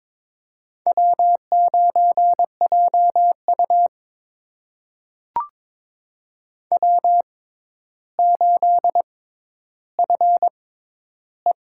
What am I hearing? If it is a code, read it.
Morse code, heard as a keyed tone, about 22 wpm, spaced as for 11 wpm, 700 Hz. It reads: W9JU W8FE
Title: USA Call Signs - 6x Character Spacing - 22wpm